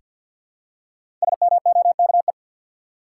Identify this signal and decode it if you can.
Morse code, heard as a keyed tone, about 47 wpm, 700 Hz. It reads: SMOKE
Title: Five Letter Words - 50wpm